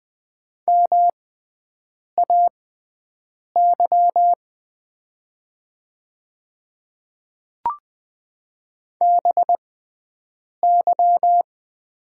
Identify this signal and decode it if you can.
Morse code, heard as a keyed tone, about 20 wpm, spaced as for 7 wpm, 700 Hz. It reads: MAY BY
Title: 6x Character Spacing - Top 100 Words - 20wpm